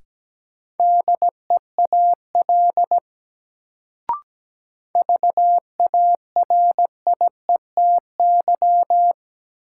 Morse code, heard as a keyed tone, about 17 wpm, 700 Hz. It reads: DEAL VARIETY
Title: Sets of 3 Words - New Word Drill - 17wpm